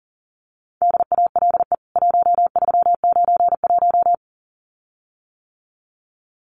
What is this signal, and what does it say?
Morse code, heard as a keyed tone, about 40 wpm, 700 Hz. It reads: DALE 1391
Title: CWT Exchange 40wpm